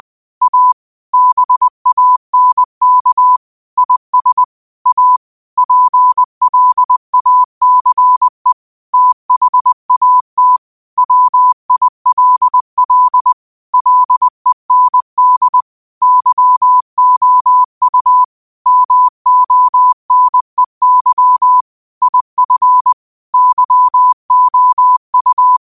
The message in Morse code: A BANK IS A PLACE THAT WILL LEND YOU MONEY IF YOU

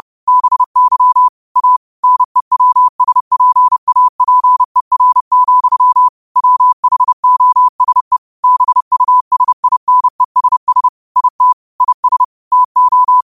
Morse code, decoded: BY A NEWSPAPER, WHOSE BUSINESS IT IS TO

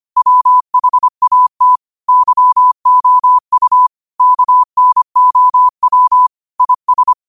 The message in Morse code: WHAT YOU KNOW IS

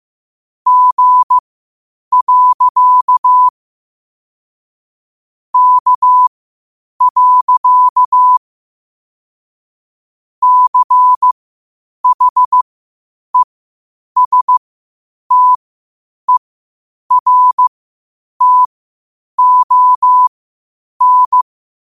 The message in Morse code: G. K. CHESTERTON

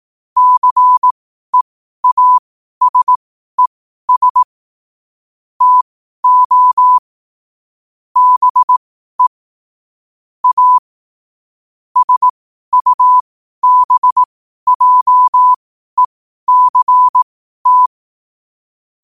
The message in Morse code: CEASES TO BE A SUBJECT

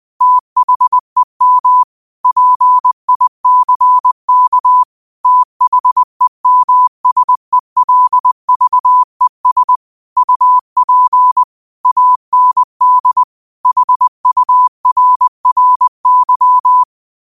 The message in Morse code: THEM PICK THEMSELVES UP AND HURRY

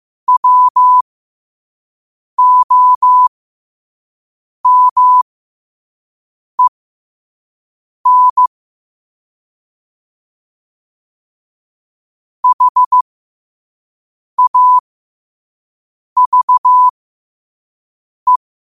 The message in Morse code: WOMEN HAVE

